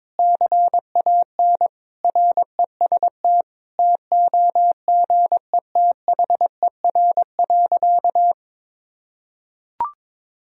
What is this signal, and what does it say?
Morse code, heard as a keyed tone, about 22 wpm, 700 Hz. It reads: CAN REST TOGETHER.
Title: Sentences from Top 500 Words and Abbreviations 22wpm